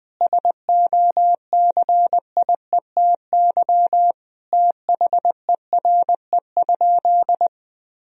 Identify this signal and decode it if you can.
Morse code, heard as a keyed tone, about 21 wpm, 700 Hz. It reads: SOCIETY THERE?
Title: Sentences from Top 1000 Words - Review Part 4 - 20wpm